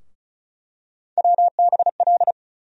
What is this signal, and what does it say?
Morse code, heard as a keyed tone, about 35 wpm, 700 Hz. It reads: WBL
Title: USA Call Sign Postfixes - 3 Letters 35wpm